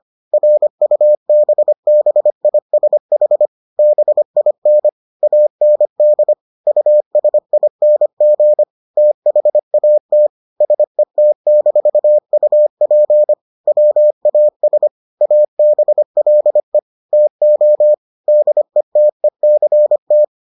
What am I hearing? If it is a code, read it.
Morse code, heard as a keyed tone, about 25 wpm, 600 Hz. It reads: RUBBISH BIN AND USING THAT SET-UP WAS ABLE TO DETECT